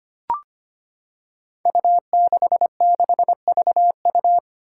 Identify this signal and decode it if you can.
Morse code, heard as a keyed tone, about 25 wpm, 700 Hz. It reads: U66VU
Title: Five Challenging Letter-Number Combinations 25wpm